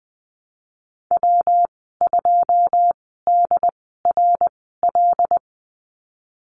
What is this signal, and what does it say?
Morse code, heard as a keyed tone, about 20 wpm, spaced as for 15 wpm, 700 Hz. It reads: W2DRL
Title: 147 Lesson 24g - Letter y - Call Signs 20wpm